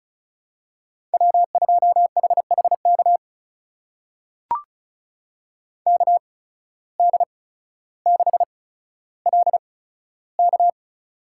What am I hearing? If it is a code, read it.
Morse code, heard as a keyed tone, about 36 wpm, spaced as for 15 wpm, 700 Hz. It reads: W2HHK KD6LK
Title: USA Call Signs - 5 Characters - 8x Character Spacing - 35wpm